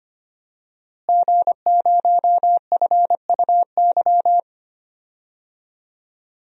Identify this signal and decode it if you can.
Morse code, heard as a keyed tone, about 25 wpm, 700 Hz. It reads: G0FUY
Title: DXCC-100 Call Signs - 5 Characters - Repeated 2x - 25wpm